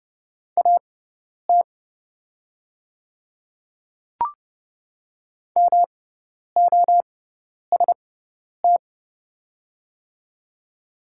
Morse code, heard as a keyed tone, about 30 wpm, spaced as for 10 wpm, 700 Hz. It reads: AT MOST
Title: 6x Character Spacing - Top 300 Words - 30wpm